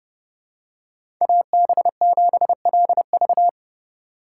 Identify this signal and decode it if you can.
Morse code, heard as a keyed tone, about 30 wpm, 700 Hz. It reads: AB7LV
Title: USA Call Signs - 4x Character Spacing - 30wpm